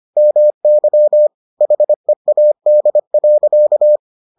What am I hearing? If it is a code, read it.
Morse code, heard as a keyed tone, about 25 wpm, 600 Hz. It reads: MY HEAD.